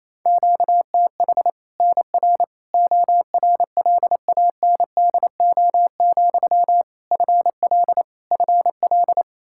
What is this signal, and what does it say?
Morse code, heard as a keyed tone, about 28 wpm, 700 Hz. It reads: QTH NR ORLANDO, FL FL
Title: Practice QSO - Exchange 1 - Beginning - 28wpm